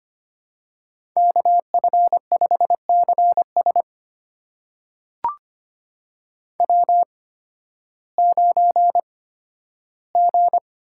Morse code, heard as a keyed tone, about 25 wpm, spaced as for 13 wpm, 700 Hz. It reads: KF5CS W9G
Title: USA Call Signs - 5 Characters - 8x Character Spacing - 25wpm